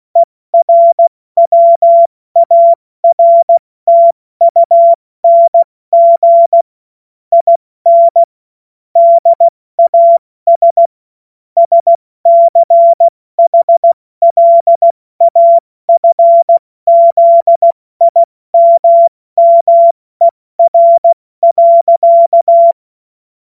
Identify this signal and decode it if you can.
Morse code, heard as a keyed tone, about 16 wpm, 680 Hz. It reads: ERWARTUNG IN DAS SCHLAFZIMMER.
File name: Morsekurs / Witze / Witz 09 April 2024 70 F80